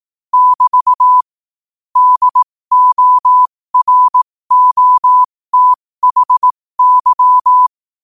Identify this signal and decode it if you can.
Morse code, heard as a keyed tone, about 18 wpm, spaced as for 15 wpm, 1000 Hz. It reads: <BT> DOROTHY
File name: qotdfiles / 15wpm